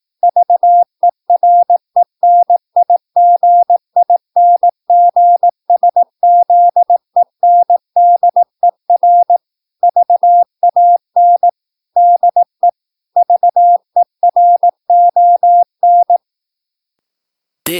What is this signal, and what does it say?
Morse code, heard as a keyed tone, about 18 wpm, 700 Hz. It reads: VERENIGINGSZENDER VAN DE VERON
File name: PI4AA Uitzending